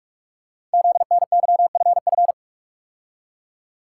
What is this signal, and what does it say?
Morse code, heard as a keyed tone, about 44 wpm, 700 Hz. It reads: ZNYFF
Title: Code Groups - 5 Letters - 45wpm